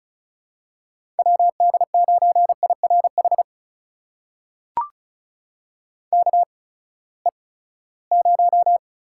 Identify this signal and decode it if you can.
Morse code, heard as a keyed tone, about 36 wpm, spaced as for 20 wpm, 700 Hz. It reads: WD9IRH KE0
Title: USA Call Signs - 6 Characters - 8x Character Spacing - 35wpm